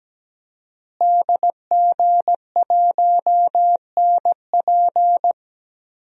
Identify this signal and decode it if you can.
Morse code, heard as a keyed tone, about 17 wpm, 700 Hz. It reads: DG1NP
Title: DXCC-100 Call Signs - 5 Characters - Repeated 2x - 17wpm